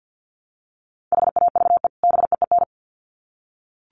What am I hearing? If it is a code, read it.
Morse code, heard as a keyed tone, about 50 wpm, 700 Hz. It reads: HAVE BEEN
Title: Sets of 2 Words 50wpm